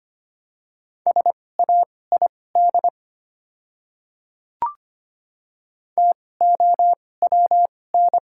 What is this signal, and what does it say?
Morse code, heard as a keyed tone, about 25 wpm, spaced as for 18 wpm, 690 Hz. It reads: SAID TOWN
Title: 2x Character Spacing - Top 500 Words - 25wpm